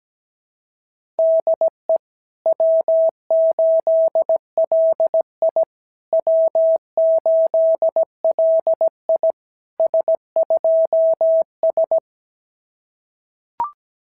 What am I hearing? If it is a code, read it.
Morse code, heard as a keyed tone, about 16 wpm, 650 Hz. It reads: DE W8LI W8LI S2S
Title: SOTA QSO Practice 17wpm